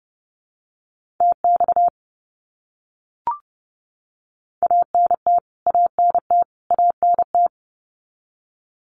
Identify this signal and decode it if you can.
Morse code, heard as a keyed tone, about 30 wpm, 700 Hz. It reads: TX ANT ANT ANT
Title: Abbreviations Repeated 3x - 30wpm